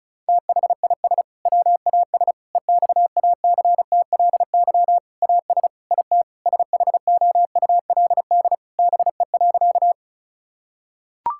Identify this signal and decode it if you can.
Morse code, heard as a keyed tone, about 34 wpm, 700 Hz. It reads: THIS WAS EXACTLY AS IT SHOULD BE.
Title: Sentences from Top 1000 Words - Review Part 6 - 35wpm